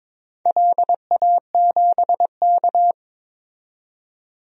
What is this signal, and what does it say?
Morse code, heard as a keyed tone, about 22 wpm, 700 Hz. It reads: LA7K
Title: DXCC-100 Call Signs - 4 Characters - 22wpm